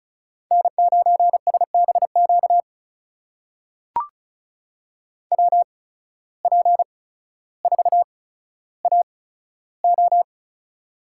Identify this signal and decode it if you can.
Morse code, heard as a keyed tone, about 34 wpm, spaced as for 15 wpm, 690 Hz. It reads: N9SBQ WP4AO
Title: USA Call Signs - 5 Characters - 8x Character Spacing - 35wpm